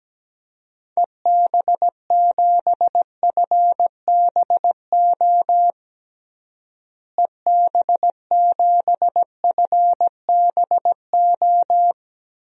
Morse code, heard as a keyed tone, about 17 wpm, 700 Hz. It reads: EB7FBO EB7FBO
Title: DXCC-100 Call Signs - 6 Characters - Repeated 2x - 17wpm